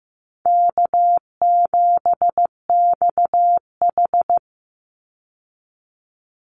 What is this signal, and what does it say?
Morse code, heard as a keyed tone, about 15 wpm, 700 Hz. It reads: K7XH
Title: Call Signs Repeated 2x - 15wpm